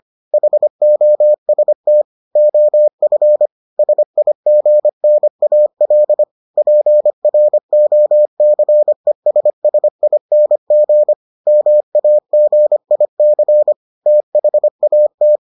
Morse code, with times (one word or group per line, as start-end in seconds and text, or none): 0.3-2.0 HOST
2.4-3.5 OF
3.8-6.2 SIGNAL
6.6-11.1 PROCESSING
11.5-13.7 MAGIC
14.1-15.4 THAT